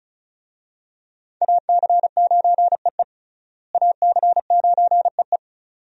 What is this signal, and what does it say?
Morse code, heard as a keyed tone, about 36 wpm, 700 Hz. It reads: AC9EE AC9EE
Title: USA Call Signs - 5 Characters - Repeated 3x - 35wpm